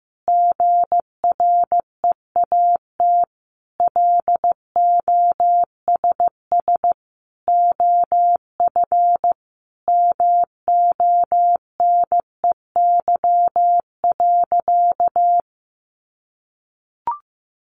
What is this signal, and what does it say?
Morse code, heard as a keyed tone, about 15 wpm, 700 Hz. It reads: GREAT LOSS OF MONEY.
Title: Sentences from Top 1400 Words 15wpm